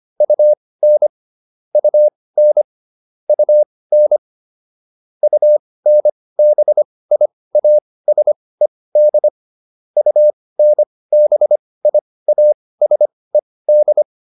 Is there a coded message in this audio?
Yes: Morse code from 0.2 to 14.0 s, heard as a keyed tone, about 25 wpm, spaced as for 18 wpm, 600 Hz. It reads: UN UN UN UNBIASED UNBIASED